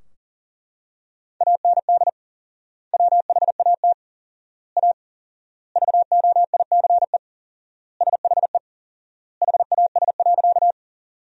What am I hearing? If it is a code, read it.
Morse code, heard as a keyed tone, about 40 wpm, 700 Hz. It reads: AND WHAT A VOICE SHE HAS.
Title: 4x Word-Spacing - Sentences from Top 500 Words - 40wpm